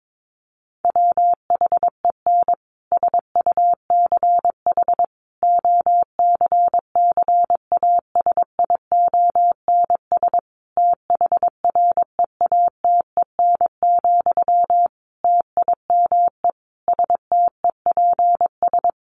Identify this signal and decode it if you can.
Morse code, heard as a keyed tone, about 22 wpm, 700 Hz. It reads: WHEN SUCH OCCASIONS THREATEN, TIME STEPS
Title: A Sound Of Thunder 22wpm